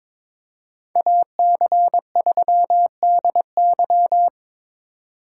Morse code, heard as a keyed tone, about 25 wpm, 700 Hz. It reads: AC3DY
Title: Speed Racing - USA 2x2 Call Signs 22wpm